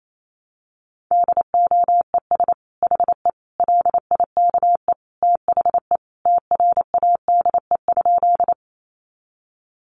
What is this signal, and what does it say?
Morse code, heard as a keyed tone, about 28 wpm, 700 Hz. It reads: DOES HE LIKE THE TRADE?
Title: Sentences from Top 600 Words 28wpm